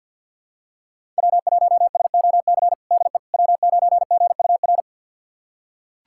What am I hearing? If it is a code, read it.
Morse code, heard as a keyed tone, about 48 wpm, 700 Hz. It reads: W1SOC DE W9GUR
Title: USA Callsign DE Callsign - 5 Characters - 50wpm